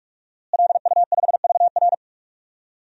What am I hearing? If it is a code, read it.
Morse code, heard as a keyed tone, about 46 wpm, 700 Hz. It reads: LU5VR
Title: DXCC-100 Call Signs - 5 Characters - Repeated 3x - 45wpm